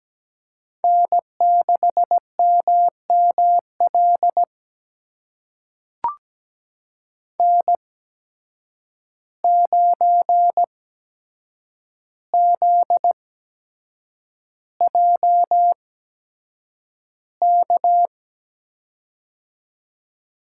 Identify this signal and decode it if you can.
Morse code, heard as a keyed tone, about 17 wpm, spaced as for 7 wpm, 700 Hz. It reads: N6MML N9ZJK
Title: USA Call Signs - 5 Characters - 8x Character Spacing - 17wpm